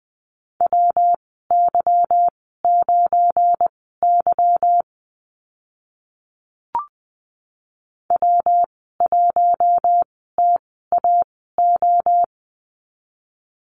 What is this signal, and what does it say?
Morse code, heard as a keyed tone, about 20 wpm, spaced as for 14 wpm, 700 Hz. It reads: WY9Y W1TAO
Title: 193 Lesson 31g - Letter v - Call Signs 20wpm